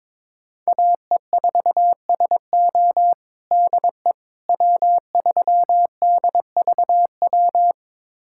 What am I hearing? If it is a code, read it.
Morse code, heard as a keyed tone, about 22 wpm, 700 Hz. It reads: AE4SO DE W3DVW
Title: USA Callsign DE Callsign - 5 Characters - 22wpm